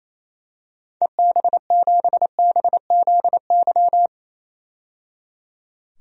Morse code, heard as a keyed tone, about 28 wpm, 700 Hz. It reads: EB7BZY
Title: DXCC-100 Call Signs - 6 Characters - 28wpm